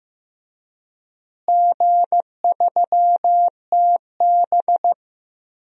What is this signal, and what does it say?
Morse code, heard as a keyed tone, about 15 wpm, 700 Hz. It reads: G3TB
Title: DXCC-100 Call Signs - 4 Characters - Repeated 2x - 15wpm